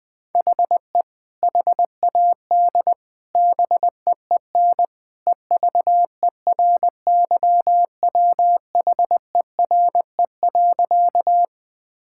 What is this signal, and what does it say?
Morse code, heard as a keyed tone, about 20 wpm, 700 Hz. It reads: HE HAD BEEN EVERYWHERE.
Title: Sentences from Top 1600 Words 20wpm